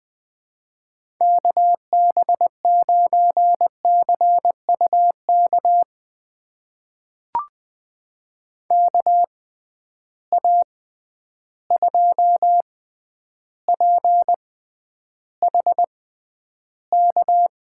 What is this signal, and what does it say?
Morse code, heard as a keyed tone, about 21 wpm, spaced as for 10 wpm, 700 Hz. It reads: KB9CUK KA2PHK
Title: USA Call Signs - 6 Characters - 6x Character Spacing - 20wpm